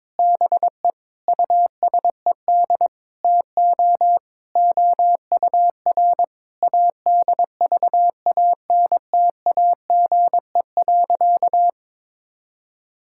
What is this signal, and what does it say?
Morse code, heard as a keyed tone, about 22 wpm, 700 Hz. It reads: BE USED TO OUR ADVANTAGE.
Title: Sentences from Top 1000 Words - Review Part 6 - 22wpm